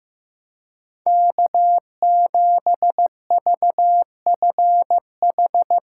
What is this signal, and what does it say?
Morse code, heard as a keyed tone, about 16 wpm, 700 Hz. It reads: K7VFH
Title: USA Callsign DE Callsign - 5 Characters - 15wpm